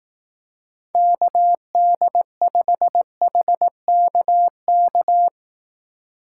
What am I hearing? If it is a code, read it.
Morse code, heard as a keyed tone, about 18 wpm, 700 Hz. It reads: KD5HKK
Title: Speed Racing - USA 2x3 Call Signs 15wpm